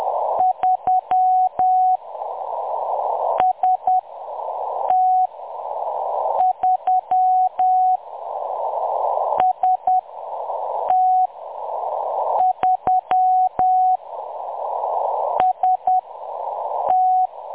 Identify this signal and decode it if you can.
Morse code, heard as a keyed tone, about 10 wpm, 740 Hz. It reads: 3 S T 3 S T 3 S T
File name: M23-ST3- -0858utc-10184khz